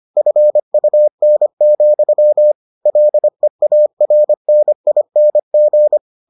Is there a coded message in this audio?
Yes: Morse code from 0.2 to 6.0 s, heard as a keyed tone, about 25 wpm, 600 Hz. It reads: FUN, LEARNING